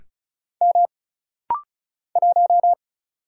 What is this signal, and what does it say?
Morse code, heard as a keyed tone, about 33 wpm, 700 Hz. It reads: M 1
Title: Single Letter-Number - Rapid-Fire 35wpm